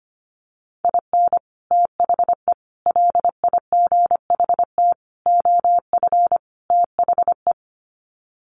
Morse code, heard as a keyed tone, about 25 wpm, 700 Hz. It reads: IN THE LIGHT OF THE
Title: Sets of 5 Words Encore2 25wpm